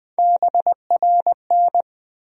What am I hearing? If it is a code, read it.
Morse code, heard as a keyed tone, about 20 wpm, 700 Hz. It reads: BRN
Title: USA Call Sign Postfixes - 3 Letters 20wpm